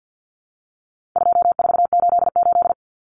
Morse code, heard as a keyed tone, about 50 wpm, 700 Hz. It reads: 2488
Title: Numbers - 4-Digit 50wpm